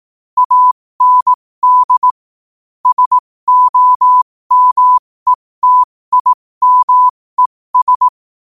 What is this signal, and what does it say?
Morse code, heard as a keyed tone, about 18 wpm, spaced as for 15 wpm, 1000 Hz. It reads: AND SOMETIMES
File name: qotdfiles / 15wpm